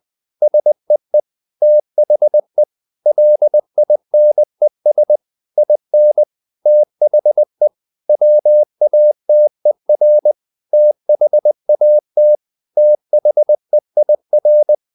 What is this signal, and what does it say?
Morse code, heard as a keyed tone, about 20 wpm, 600 Hz. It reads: SEE THE LINES IN THE WATER THAT THEIR
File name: the old man and the sea part 2 20wpm